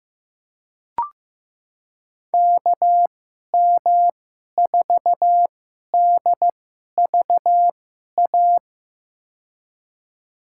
Morse code, heard as a keyed tone, about 15 wpm, spaced as for 11 wpm, 700 Hz. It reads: KM4DVA
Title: USA Call Signs - 6 Characters - 2x Character Spacing - 15wpm